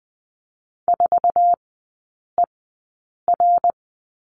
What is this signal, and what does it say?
Morse code, heard as a keyed tone, about 20 wpm, spaced as for 8 wpm, 700 Hz. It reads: 4ER
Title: 053 Lesson 10e - Letter r - Three Characters 20wpm